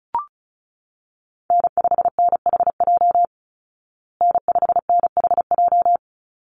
Morse code, heard as a keyed tone, about 34 wpm, 700 Hz. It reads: N5NHJ N5NHJ
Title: Call Signs Repeated 3x - 35wpm